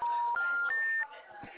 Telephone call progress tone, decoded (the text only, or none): special information tone